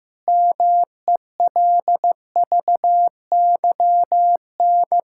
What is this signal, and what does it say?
Morse code, heard as a keyed tone, about 16 wpm, 700 Hz. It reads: MELVYN
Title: K1USN SST QSO Practice - 15wpm